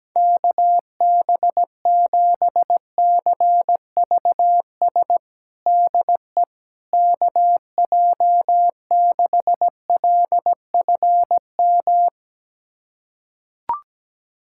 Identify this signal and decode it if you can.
Morse code, heard as a keyed tone, about 17 wpm, 700 Hz. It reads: KB7CVS DE KJ6LFM
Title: USA Callsign DE Callsign - 6 Characters - 17wpm